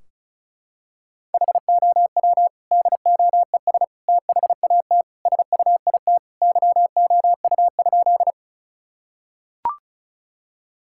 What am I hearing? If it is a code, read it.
Morse code, heard as a keyed tone, about 36 wpm, 700 Hz. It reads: HOW DOES THAT SUIT YOU?
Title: Sentences from Top 1600 Words 35wpm